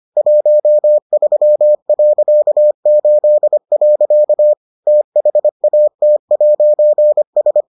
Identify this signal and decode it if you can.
Morse code, heard as a keyed tone, about 25 wpm, 600 Hz. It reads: 13.8. THAT'S